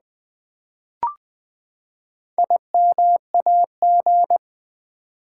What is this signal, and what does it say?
Morse code, heard as a keyed tone, about 20 wpm, 700 Hz. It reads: IMAG
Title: Build A Word - 4 Letters At A Time - 11-Letter Words - 20wpm